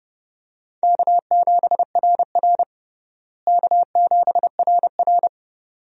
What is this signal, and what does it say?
Morse code, heard as a keyed tone, about 30 wpm, 700 Hz. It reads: K7RR K7RR
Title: USA Call Signs - 4 Characters - Repeated 2x - 30wpm